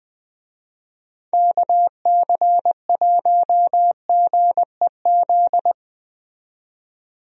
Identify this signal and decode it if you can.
Morse code, heard as a keyed tone, about 21 wpm, 700 Hz. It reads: KC1GEZ
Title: USA Call Signs - 6 Characters - 20wpm